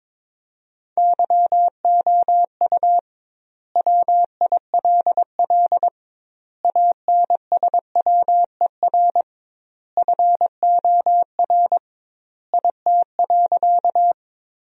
Morse code, heard as a keyed tone, about 22 wpm, 700 Hz. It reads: YOU WILL ANSWER FOR IT.
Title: 2x Word-Spacing - Sentences from Top 300 Words - 22wpm